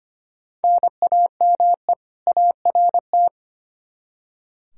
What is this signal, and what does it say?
Morse code, heard as a keyed tone, about 25 wpm, 700 Hz. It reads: NAME ART
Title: K1USN SST QSO Practice 25wpm